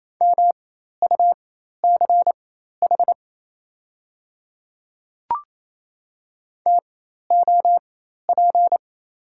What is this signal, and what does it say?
Morse code, heard as a keyed tone, about 28 wpm, spaced as for 13 wpm, 700 Hz. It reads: MUCH TOP
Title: 4x Character Spacing - Top 500 Words - 28wpm